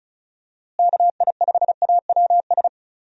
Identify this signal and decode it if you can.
Morse code, heard as a keyed tone, about 36 wpm, 700 Hz. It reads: KI5AWS
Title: USA Call Signs - 6 Characters - Repeated 2x - 35wpm